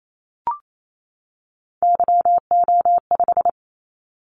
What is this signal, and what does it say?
Morse code, heard as a keyed tone, about 28 wpm, 700 Hz. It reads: YO5
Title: Callsign Prefixes for Top 25 DXCC - 28wpm